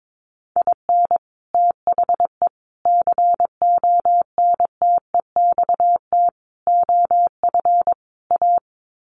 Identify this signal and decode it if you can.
Morse code, heard as a keyed tone, about 22 wpm, 700 Hz. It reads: IN THE CONTEXT OF A
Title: Sets of 5 Words Encore 22wpm